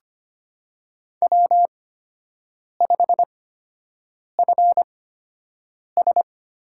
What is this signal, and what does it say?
Morse code, heard as a keyed tone, about 25 wpm, spaced as for 7 wpm, 700 Hz. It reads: W5FS
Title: USA Call Signs - 8x Character Spacing - 25wpm